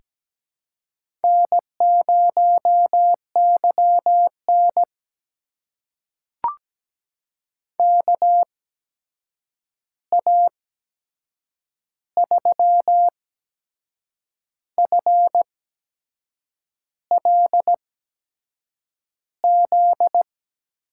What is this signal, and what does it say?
Morse code, heard as a keyed tone, about 17 wpm, spaced as for 6 wpm, 700 Hz. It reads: N0YN KA3FLZ
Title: USA Call Signs - 8x Character Spacing - 17wpm